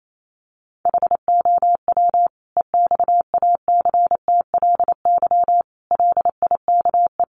Morse code, heard as a keyed tone, about 28 wpm, 700 Hz. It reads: HOW EXACTLY LIKE
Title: Sentences from Top 900 Words 28wpm